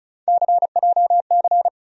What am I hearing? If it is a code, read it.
Morse code, heard as a keyed tone, about 36 wpm, 700 Hz. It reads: CJC
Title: USA Call Sign Postfixes - 3 Letters 35wpm